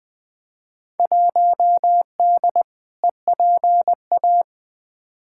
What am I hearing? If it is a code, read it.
Morse code, heard as a keyed tone, about 20 wpm, 700 Hz. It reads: 1D EPA
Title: Field Day Exchanges - 20wpm